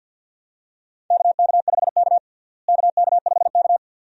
Morse code, heard as a keyed tone, about 51 wpm, 700 Hz. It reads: KK5K KK5K
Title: USA Call Signs - 4 Characters - Repeated 2x - 50wpm